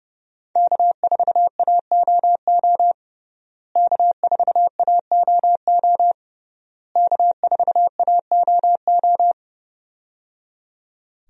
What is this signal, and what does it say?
Morse code, heard as a keyed tone, about 30 wpm, 700 Hz. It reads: K4AOO K4AOO K4AOO
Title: USA Call Signs - 5 Characters - Repeated 3x - 30wpm